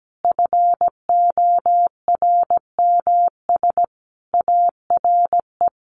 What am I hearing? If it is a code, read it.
Morse code, heard as a keyed tone, about 17 wpm, 700 Hz. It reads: FORMS ARE
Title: A Sound Of Thunder 17wpm